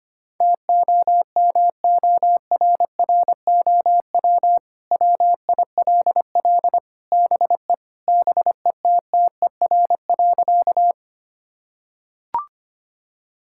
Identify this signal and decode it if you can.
Morse code, heard as a keyed tone, about 25 wpm, 700 Hz. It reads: TOMORROW WILL BE BETTER.
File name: practice / 25wpm / Sentences with 2-Word Phrases - 25wpm